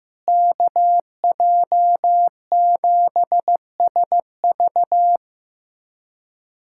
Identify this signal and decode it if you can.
Morse code, heard as a keyed tone, about 15 wpm, 700 Hz. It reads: KJ7SV
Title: USA Callsign DE Callsign - 5 Characters - 15wpm